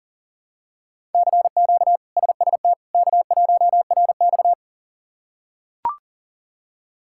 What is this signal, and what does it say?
Morse code, heard as a keyed tone, about 40 wpm, 700 Hz. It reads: CQ SST K1RX
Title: K1USN SST QSO Practice - 40wpm